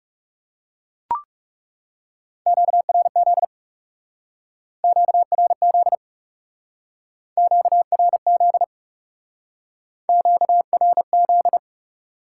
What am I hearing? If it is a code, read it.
Morse code, heard as a keyed tone, about 37 wpm, 700 Hz. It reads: QRZ QRZ QRZ QRZ
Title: Speed-Racing Top 90 QSO Elements 30wpm